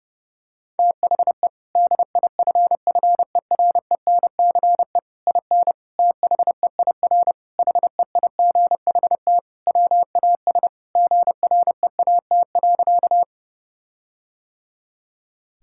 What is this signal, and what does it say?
Morse code, heard as a keyed tone, about 30 wpm, 700 Hz. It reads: THE DIFFERENCE IN THEIR HEIGHT WAS GREAT.
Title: Sentences from Top 1600 Words 30wpm